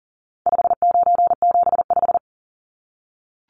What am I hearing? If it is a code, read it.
Morse code, heard as a keyed tone, about 40 wpm, 700 Hz. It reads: 5975
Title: Numbers - 4-Digit 40wpm